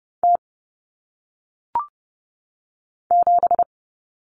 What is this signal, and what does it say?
Morse code, heard as a keyed tone, about 30 wpm, 700 Hz. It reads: T 7
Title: Common Cut Numbers - 1-Digit 30wpm